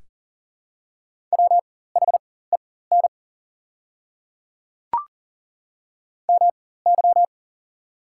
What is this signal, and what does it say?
Morse code, heard as a keyed tone, about 40 wpm, spaced as for 19 wpm, 700 Hz. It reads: WHEN MY
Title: 4x Character Spacing - Top 200 Words - 40wpm